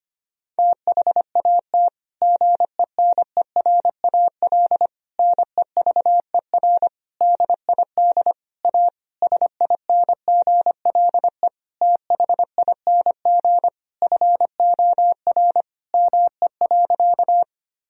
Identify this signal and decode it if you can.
Morse code, heard as a keyed tone, about 25 wpm, 700 Hz. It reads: THAT GENERAL NEVER DID A SINGLE THING FOR ME.
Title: Sentences from Top 1000 Words - Review Part 4 - 25wpm